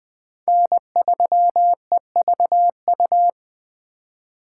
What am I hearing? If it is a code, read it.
Morse code, heard as a keyed tone, about 20 wpm, 700 Hz. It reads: N3EVU
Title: USA Call Signs - 5 Characters - 2x Character Spacing - 20wpm